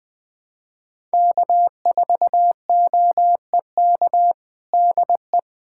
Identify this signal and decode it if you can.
Morse code, heard as a keyed tone, about 21 wpm, 700 Hz. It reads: K4OEK DE
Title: USA Callsign DE Callsign 20wpm